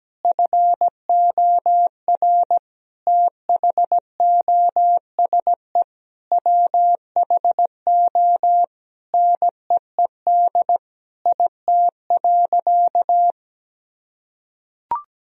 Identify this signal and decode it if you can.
Morse code, heard as a keyed tone, about 17 wpm, 700 Hz. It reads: FOR THOSE WHO NEED IT.
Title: Sentences from Top 1000 Words - Review Part 4 - 17wpm